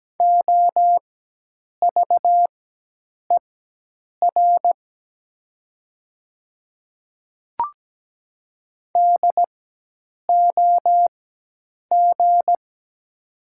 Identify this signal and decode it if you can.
Morse code, heard as a keyed tone, about 17 wpm, spaced as for 8 wpm, 690 Hz. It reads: OVER DOG
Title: 4x Character Spacing - Top 500 Words - 17wpm